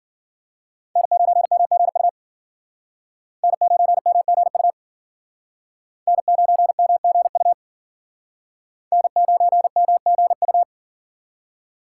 Morse code, heard as a keyed tone, about 51 wpm, 700 Hz. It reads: N9MGU N9MGU N9MGU N9MGU